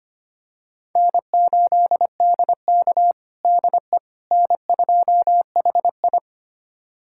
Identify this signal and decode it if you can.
Morse code, heard as a keyed tone, about 25 wpm, 700 Hz. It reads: N8DK DE N2HI
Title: USA Callsign DE Callsign - 4 Characters - 25wpm